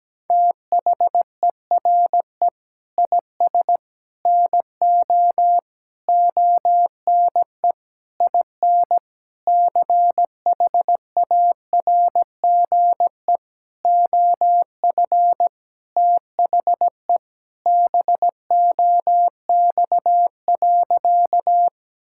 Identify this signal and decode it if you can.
Morse code, heard as a keyed tone, about 17 wpm, 700 Hz. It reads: THERE IS NO ONE IN CHARGE OF THE BOX.
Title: Sentences from Top 1000 Words - Review Part 6 - 17wpm